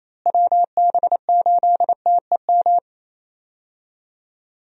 Morse code, heard as a keyed tone, about 28 wpm, 700 Hz. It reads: WB8TEM
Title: USA Call Signs - 6 Characters - Repeated 3x - 28wpm